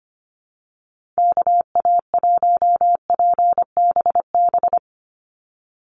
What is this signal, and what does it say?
Morse code, heard as a keyed tone, about 25 wpm, 700 Hz. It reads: KA1PBB
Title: USA Call Signs 25wpm